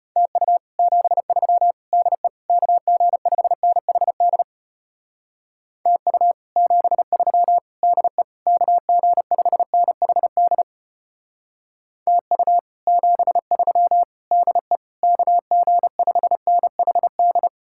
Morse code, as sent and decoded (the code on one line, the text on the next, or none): - ..- / --... ...-- / -.. / -.- --. ..... -. .... -.. / - ..- / --... ...-- / -.. / -.- --. ..... -. .... -.. / - ..- / --... ...-- / -.. / -.- --. ..... -. .... -..
TU 73 DE KG5NHD TU 73 DE KG5NHD TU 73 DE KG5NHD